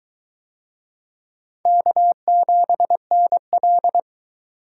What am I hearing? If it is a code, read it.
Morse code, heard as a keyed tone, about 21 wpm, 700 Hz. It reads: K7NL